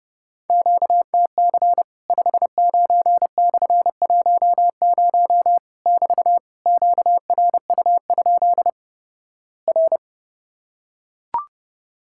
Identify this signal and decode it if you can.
Morse code, heard as a keyed tone, about 30 wpm, 700 Hz. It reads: QTC 59/10 <BT> QRU? R
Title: Worked All Europe DX Contest - QTC Exchange - 30wpm